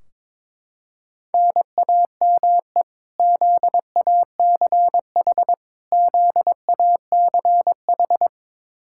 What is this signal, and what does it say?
Morse code, heard as a keyed tone, about 21 wpm, 700 Hz. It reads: NAME ZACH ZACH
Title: Practice QSO - Exchange 1 - Beginning - 22wpm